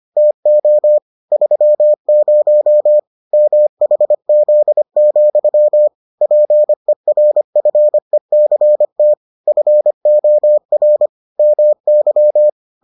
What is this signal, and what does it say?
Morse code, heard as a keyed tone, about 25 wpm, 600 Hz. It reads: TO 30 MHZ, PERFECT FOR MY